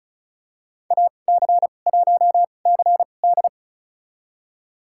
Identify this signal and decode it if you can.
Morse code, heard as a keyed tone, about 34 wpm, spaced as for 25 wpm, 700 Hz. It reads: AC1CD
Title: USA Call Signs - 2x Character Spacing - 35wpm